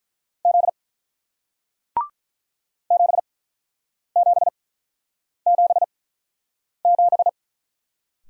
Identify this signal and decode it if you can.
Morse code, heard as a keyed tone, about 48 wpm, spaced as for 9 wpm, 700 Hz. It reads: Z 7777